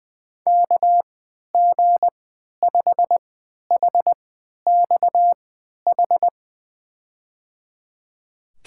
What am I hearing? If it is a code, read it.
Morse code, heard as a keyed tone, about 20 wpm, spaced as for 11 wpm, 700 Hz. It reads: KG5HXH
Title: USA Call Signs - 6 Characters - 3x Character Spacing - 20wpm